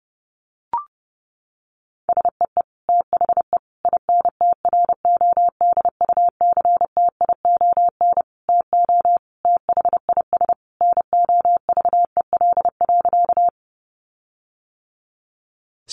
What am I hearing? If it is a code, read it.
Morse code, heard as a keyed tone, about 30 wpm, 700 Hz. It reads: SEE THE INTRODUCTION TO THIS NOVEL.
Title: Sentences from Top 1300 Words 30wpm